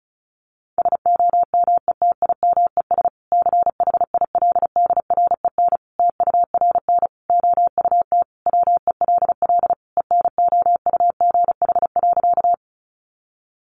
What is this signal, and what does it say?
Morse code, heard as a keyed tone, about 34 wpm, 700 Hz. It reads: SOMETIMES CHILDREN TURN OUT WELL ENOUGH.
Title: Sentences from Top 600 Words 35wpm